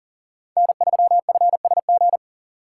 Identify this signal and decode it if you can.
Morse code, heard as a keyed tone, about 40 wpm, 700 Hz. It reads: N3FSG
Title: USA Call Signs - 5 Characters - 2x Character Spacing - 40wpm